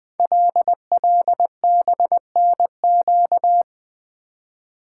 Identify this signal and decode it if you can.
Morse code, heard as a keyed tone, about 20 wpm, 700 Hz. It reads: LLBNQ
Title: Code Groups - 5 Letters - 20wpm